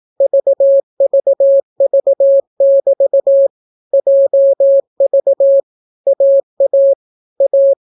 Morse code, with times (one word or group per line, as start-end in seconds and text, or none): 0.2-3.5 VVV<BT>
3.9-5.6 JV
6.1-6.9 AA
7.4-7.7 A